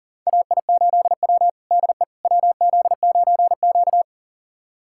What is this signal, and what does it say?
Morse code, heard as a keyed tone, about 40 wpm, 700 Hz. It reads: AI8W DE WZ9Q
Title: USA Callsign DE Callsign - 4 Characters - 40wpm